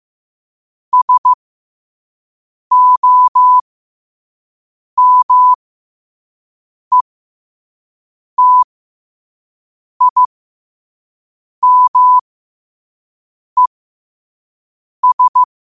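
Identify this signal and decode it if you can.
Morse code, heard as a keyed tone, about 15 wpm, spaced as for 5 wpm, 1000 Hz. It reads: SOMETIMES
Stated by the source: Morse code sent at 5 wpm.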